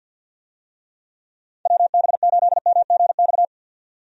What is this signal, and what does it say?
Morse code, heard as a keyed tone, about 52 wpm, 700 Hz. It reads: WB8MGX
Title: USA Call Signs - 6 Characters - 50wpm